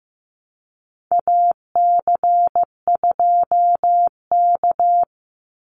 Morse code, heard as a keyed tone, about 15 wpm, 700 Hz. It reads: AC2K
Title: Call Signs Repeated 3x - 15wpm